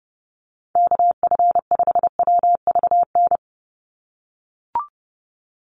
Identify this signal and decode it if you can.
Morse code, heard as a keyed tone, about 30 wpm, 700 Hz. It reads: KF5WVN
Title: USA Call Signs 30wpm